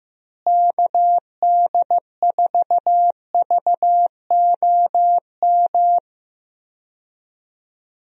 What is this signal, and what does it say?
Morse code, heard as a keyed tone, about 15 wpm, 700 Hz. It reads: KD4VOM
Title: USA Call Signs - 6 Characters - Repeated 3x - 15wpm